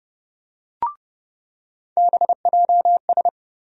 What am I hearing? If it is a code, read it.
Morse code, heard as a keyed tone, about 30 wpm, 700 Hz. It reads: BJS
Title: USA Call Sign Postfixes - 3 Letters 30wpm